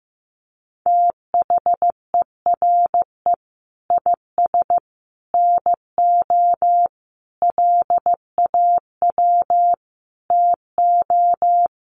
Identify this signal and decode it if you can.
Morse code, heard as a keyed tone, about 15 wpm, 700 Hz. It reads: THERE IS NO LAW TO
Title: Sentences from Top 1100 Words 15wpm